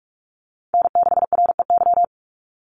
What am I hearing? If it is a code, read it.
Morse code, heard as a keyed tone, about 44 wpm, 700 Hz. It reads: N6REY
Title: USA Call Signs 45wpm